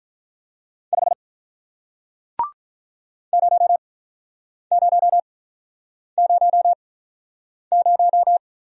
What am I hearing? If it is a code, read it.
Morse code, heard as a keyed tone, about 48 wpm, spaced as for 9 wpm, 700 Hz. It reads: 5 0000